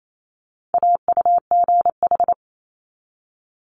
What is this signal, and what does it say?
Morse code, heard as a keyed tone, about 28 wpm, 700 Hz. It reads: AUGH
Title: Multi-letter Phonemes 28wpm